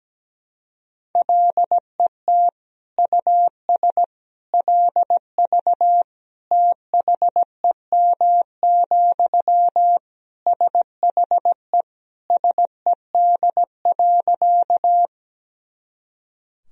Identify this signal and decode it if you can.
Morse code, heard as a keyed tone, about 17 wpm, 700 Hz. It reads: LET US LV THEM, SHE SED.
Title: Sentences from Top 400 Words and Abbreviations 17wpm